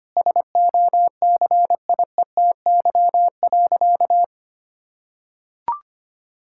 Morse code, heard as a keyed tone, about 25 wpm, 700 Hz. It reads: SOCIETY.
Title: Sentences from Top 1000 Words - Review Part 6 - 25wpm